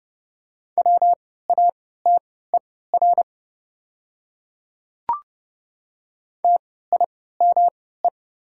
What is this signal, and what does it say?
Morse code, heard as a keyed tone, about 30 wpm, spaced as for 17 wpm, 700 Hz. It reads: WATER TIME
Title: 3x Character Spacing - Top 100 Words - 30wpm